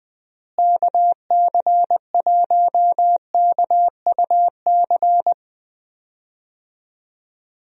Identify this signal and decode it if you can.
Morse code, heard as a keyed tone, about 20 wpm, 700 Hz. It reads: KC1KUC
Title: USA Call Signs - 6 Characters - Repeated 3x - 20wpm